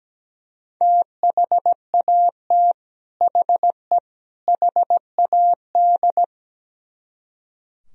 Morse code, heard as a keyed tone, about 17 wpm, 700 Hz. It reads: THAT HE HAD
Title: Speed-Racing - Sets of 3 Words 17wpm